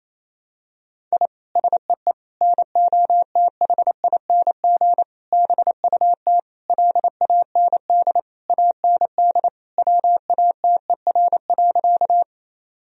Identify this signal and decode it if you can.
Morse code, heard as a keyed tone, about 28 wpm, 700 Hz. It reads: I SEE NOTHING BUT LAND AND WATER.
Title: Sentences from Top 1000 Words - Review Part 1 - 28wpm